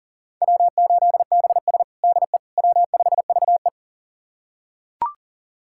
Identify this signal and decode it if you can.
Morse code, heard as a keyed tone, about 40 wpm, 700 Hz. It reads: W8BS DE W5VE
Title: USA Callsign DE Callsign - 4 Characters - 40wpm